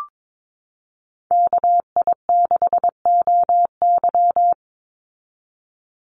Morse code, heard as a keyed tone, about 22 wpm, 700 Hz. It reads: KI6OY
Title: Call Signs Repeated 3x - 22wpm